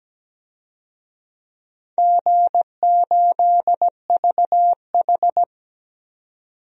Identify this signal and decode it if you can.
Morse code, heard as a keyed tone, about 17 wpm, 700 Hz. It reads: G8VH
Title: DXCC-100 Call Signs - 4 Characters - 17wpm